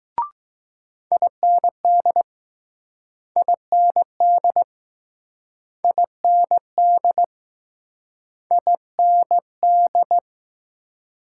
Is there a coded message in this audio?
Yes: Morse code from 1.1 to 10.2 s, heard as a keyed tone, about 20 wpm, 700 Hz. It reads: IND IND IND IND